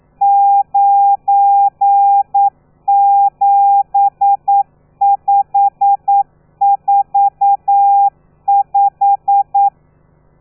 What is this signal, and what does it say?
Morse code, heard as a keyed tone, about 9 wpm, 800 Hz. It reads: 97545
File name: telegrafia / i2tfj / Lez12 40